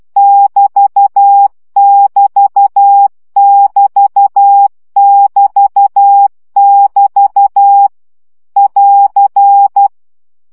Morse code, heard as a keyed tone, about 12 wpm, 800 Hz. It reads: <BT><BT><BT><BT><BT> <AR>
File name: CW gyakorlas egyenlo 10 baud